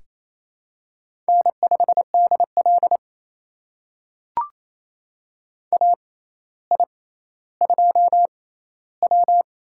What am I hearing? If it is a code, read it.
Morse code, heard as a keyed tone, about 28 wpm, spaced as for 14 wpm, 700 Hz. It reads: N5DL AI2W
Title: USA Call Signs - 4 Characters - 6x Character Spacing - 28wpm